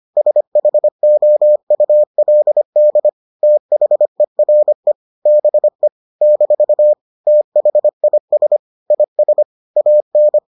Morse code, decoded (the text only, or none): SHOULD THERE BE - THIS IS AN